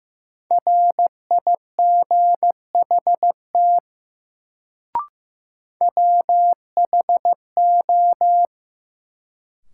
Morse code, heard as a keyed tone, about 16 wpm, 700 Hz. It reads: RIGHT WHO
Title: Sets of 2 Words - New Word Drill - 15wpm